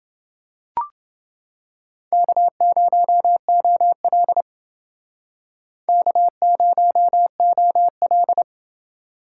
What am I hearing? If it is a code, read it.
Morse code, heard as a keyed tone, about 29 wpm, 700 Hz. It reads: K0OL K0OL